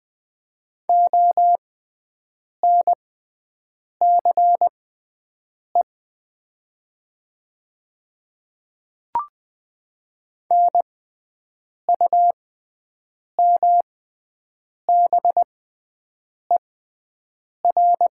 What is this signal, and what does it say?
Morse code, heard as a keyed tone, about 20 wpm, spaced as for 7 wpm, 700 Hz. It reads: ONCE NUMBER
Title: 6x Character Spacing - Top 300 Words - 20wpm